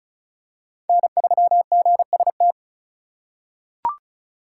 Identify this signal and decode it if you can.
Morse code, heard as a keyed tone, about 34 wpm, 700 Hz. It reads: N3GST
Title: USA Call Signs - 4x Character Spacing - 35wpm